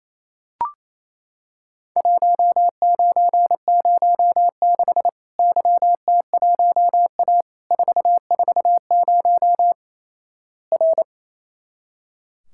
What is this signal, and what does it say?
Morse code, heard as a keyed tone, about 28 wpm, 700 Hz. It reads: 1906 YT1A 440 R
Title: Worked All Europe DX Contest - QTC Exchange - 28wpm